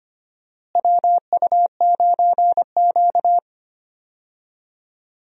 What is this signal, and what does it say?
Morse code, heard as a keyed tone, about 25 wpm, 700 Hz. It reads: WU9Q
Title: USA Call Signs - 4 Characters - Repeated 2x - 25wpm